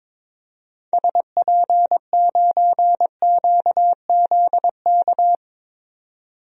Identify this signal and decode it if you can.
Morse code, heard as a keyed tone, about 22 wpm, 700 Hz. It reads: SP9QZK
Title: DXCC-100 Call Signs - 6 Characters - 22wpm